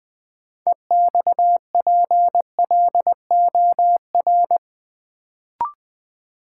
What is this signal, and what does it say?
Morse code, heard as a keyed tone, about 20 wpm, 700 Hz. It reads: EXPLOR
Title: Sentences from Top 1100 Words - New Word Drill Alt - 20wpm